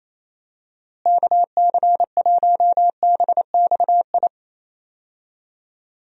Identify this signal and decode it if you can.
Morse code, heard as a keyed tone, about 28 wpm, 700 Hz. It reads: KC1BXI
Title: USA Call Signs - 6 Characters - Repeated 2x - 28wpm